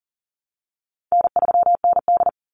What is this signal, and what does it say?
Morse code, heard as a keyed tone, about 40 wpm, 700 Hz. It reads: N3ND
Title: Easy Call Signs - 40wpm